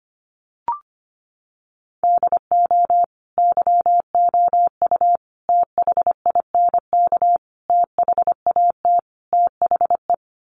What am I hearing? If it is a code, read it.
Morse code, heard as a keyed tone, about 25 wpm, 700 Hz. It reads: DO YOU THINK THAT THE
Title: Sets of 5 Words Encore 25wpm